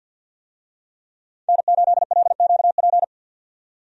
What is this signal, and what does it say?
Morse code, heard as a keyed tone, about 50 wpm, 700 Hz. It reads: N8LQP
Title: USA Call Signs - 5 Characters - 50wpm